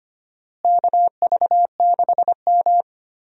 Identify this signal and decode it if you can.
Morse code, heard as a keyed tone, about 25 wpm, 700 Hz. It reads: KV6M
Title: USA Call Signs - 4 Characters - Repeated 2x - 25wpm